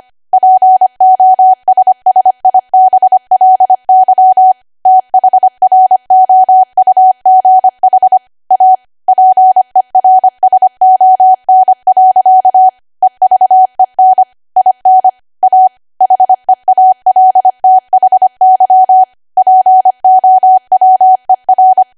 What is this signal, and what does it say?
Morse code, heard as a keyed tone, about 25 wpm, 750 Hz. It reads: POSSIBLY THROUGH A PERSON. EVEN IN A HEALTHY POWER
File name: cwa / W1AW-Level3 / 25WPM